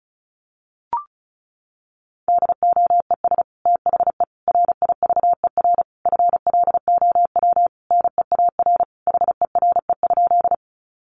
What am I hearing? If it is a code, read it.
Morse code, heard as a keyed tone, about 34 wpm, 700 Hz. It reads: DOES THE RIVER FLOW NEAR HERE?
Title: Sentences from Top 1200 Words 35wpm